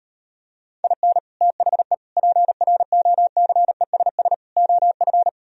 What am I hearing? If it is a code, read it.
Morse code, heard as a keyed tone, about 39 wpm, 690 Hz. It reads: IN THE PROCESS OF